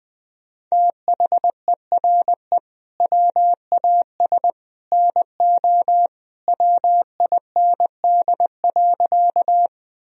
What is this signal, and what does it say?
Morse code, heard as a keyed tone, about 21 wpm, 700 Hz. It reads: THERE WAS NO WIND.
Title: Sentences from Top 1000 Words - Review Part 2 - 20wpm